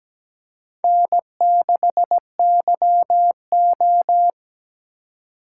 Morse code, heard as a keyed tone, about 17 wpm, 700 Hz. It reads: N6YO
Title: USA Call Signs - 4 Characters - 3x Character Spacing - 17wpm